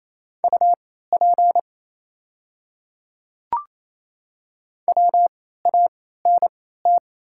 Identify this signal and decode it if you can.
Morse code, heard as a keyed tone, about 28 wpm, spaced as for 16 wpm, 700 Hz. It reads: UP WANT
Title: 3x Character Spacing - Top 200 Words - 28wpm